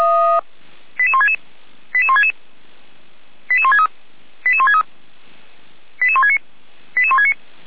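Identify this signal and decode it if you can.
Five-tone selective call (ZVEI), heard as quick runs of five tones, tones of about 70 ms. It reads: 88166 88166 88163 88163 88169 88169
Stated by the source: ZVEI five-tone selective call received on an SDR.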